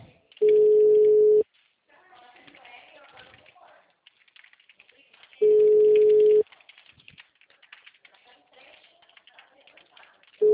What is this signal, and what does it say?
Ringback tone (Europe, 425 Hz), with speech.